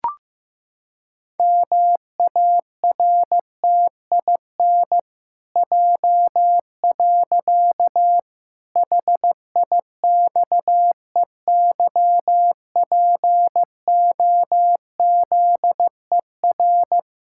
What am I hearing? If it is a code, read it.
Morse code, heard as a keyed tone, about 15 wpm, 700 Hz. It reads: MARTIN J. HIXEYPOZER
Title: Pangram Challenge v2 - 15wpm